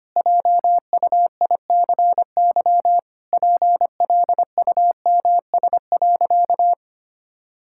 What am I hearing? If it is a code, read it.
Morse code, heard as a keyed tone, about 25 wpm, 700 Hz. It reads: JUICY PLUMS.
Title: Pangram Challenge v2 - 25wpm